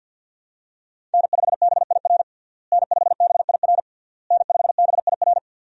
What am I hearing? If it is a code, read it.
Morse code, heard as a keyed tone, about 51 wpm, 700 Hz. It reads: N5BIR N5BIR N5BIR
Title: USA Call Signs - 5 Characters - Repeated 3x - 50wpm